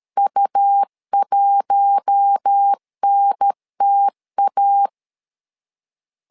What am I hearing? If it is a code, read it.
Morse code, heard as a keyed tone, about 13 wpm, 800 Hz. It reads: U1NTA
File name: LEZIONI / 4 Q1 12wpm 2028